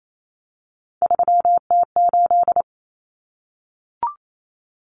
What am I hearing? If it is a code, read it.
Morse code, heard as a keyed tone, about 28 wpm, 700 Hz. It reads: 3T8
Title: Common Cut Numbers - 3-Digit 28wpm